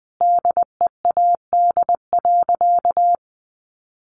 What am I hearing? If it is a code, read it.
Morse code, heard as a keyed tone, about 20 wpm, 700 Hz. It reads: DEAD.
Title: Sentences from Top 1500 Words 20wpm